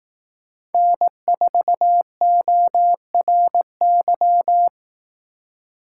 Morse code, heard as a keyed tone, about 18 wpm, 700 Hz. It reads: N4ORY
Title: Speed Racing - USA 1x3 Call Signs 15wpm